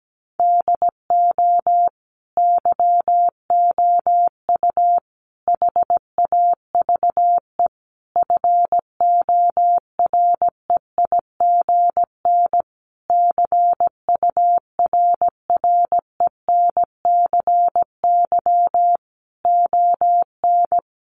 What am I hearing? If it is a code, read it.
Morse code, heard as a keyed tone, about 17 wpm, 700 Hz. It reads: DO YOU HAVE FOREIGN CURRENCY ON
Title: Sentences from Top 1500 Words 17wpm